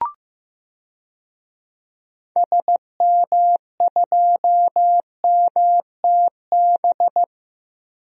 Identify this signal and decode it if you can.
Morse code, heard as a keyed tone, about 15 wpm, 700 Hz. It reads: SM2MTB
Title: DXCC-100 Call Signs - 6 Characters - Repeated 2x - 15wpm